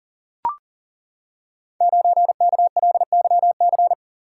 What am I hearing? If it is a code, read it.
Morse code, heard as a keyed tone, about 40 wpm, 700 Hz. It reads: 9KLYC
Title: Code Groups - 5 Letter:Numbers - 40wpm